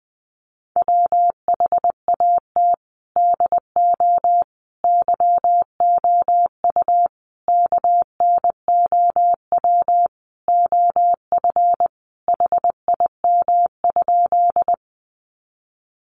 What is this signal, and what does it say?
Morse code, heard as a keyed tone, about 20 wpm, 700 Hz. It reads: WHAT DO YOU KNOW OF HIM?
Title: Sentences from Top 100 Words 20wpm